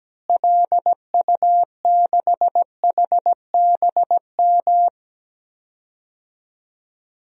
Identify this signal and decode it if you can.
Morse code, heard as a keyed tone, about 17 wpm, 700 Hz. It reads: LU6HBM
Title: DXCC-100 Call Signs - 6 Characters - Repeated 2x - 17wpm